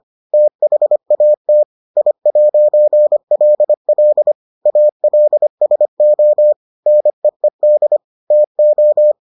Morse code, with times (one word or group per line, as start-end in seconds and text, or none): 0.3-1.6 THAT
2.0-4.3 I'LL
4.7-6.5 ALSO
6.9-8.0 NEED
8.3-9.1 TO